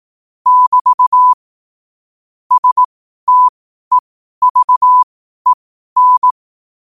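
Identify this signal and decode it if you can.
Morse code, heard as a keyed tone, about 18 wpm, spaced as for 12 wpm, 1000 Hz. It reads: <BT> STEVEN
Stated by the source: Morse code at 12 wpm.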